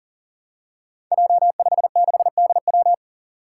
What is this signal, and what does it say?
Morse code, heard as a keyed tone, about 40 wpm, 700 Hz. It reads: J56DW
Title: Five Challenging Letter-Number Combinations 40wpm